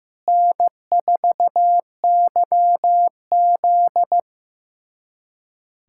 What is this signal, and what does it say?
Morse code, heard as a keyed tone, about 15 wpm, 700 Hz. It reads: N4YZ
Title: USA Callsign DE Callsign - 4 Characters - 15wpm